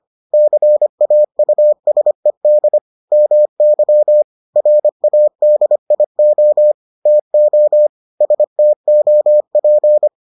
Morse code, heard as a keyed tone, about 25 wpm, 600 Hz. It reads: CAUSED MY RADIO TO STOP